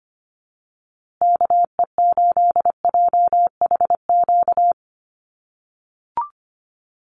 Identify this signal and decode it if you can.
Morse code, heard as a keyed tone, about 25 wpm, 700 Hz. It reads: KE8JHQ
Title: USA Call Signs Repeated 2x 25wpm